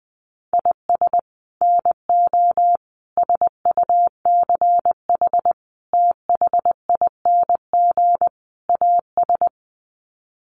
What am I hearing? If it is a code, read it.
Morse code, heard as a keyed tone, about 20 wpm, 700 Hz. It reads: IS NO SUCH THING AS
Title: Sets of 5 Words Encore 20wpm